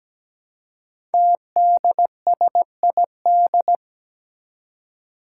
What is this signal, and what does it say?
Morse code, heard as a keyed tone, about 17 wpm, 700 Hz. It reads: TDSID
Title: Five Challenging Letter-Number Combinations 17wpm